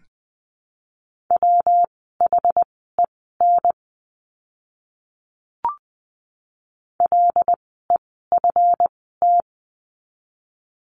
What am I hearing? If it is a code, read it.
Morse code, heard as a keyed tone, about 20 wpm, spaced as for 14 wpm, 700 Hz. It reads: WHEN LEFT
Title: 139 Lesson 23f - Letter f - Words 20wpm